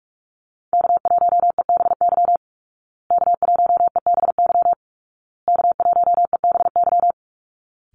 Morse code, heard as a keyed tone, about 44 wpm, 700 Hz. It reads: K1EBY K1EBY K1EBY
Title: Call Signs Repeated 2x - 45wpm